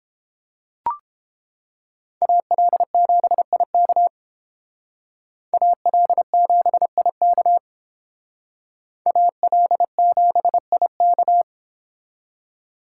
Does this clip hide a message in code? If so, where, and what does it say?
Yes: Morse code from 2.2 to 11.4 s, heard as a keyed tone, about 30 wpm, 700 Hz. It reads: AL7IK AL7IK AL7IK